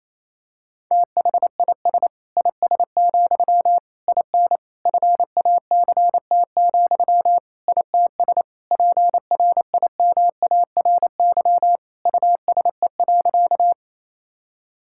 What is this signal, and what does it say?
Morse code, heard as a keyed tone, about 27 wpm, 700 Hz. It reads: THIS IS, IN FACT, ITS PRIMARY USE.
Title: Sentences from Top 1000 Words - Review Part 5 - 28wpm